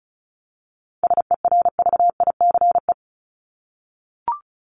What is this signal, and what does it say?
Morse code, heard as a keyed tone, about 34 wpm, 700 Hz. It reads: SERVICE
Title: ES:S Suffix - Top 100 Words with Base Words 35wpm